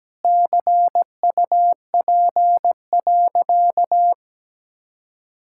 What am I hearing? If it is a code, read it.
Morse code, heard as a keyed tone, about 17 wpm, 700 Hz. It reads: CUP.
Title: Sentences from Top 1600 Words 17wpm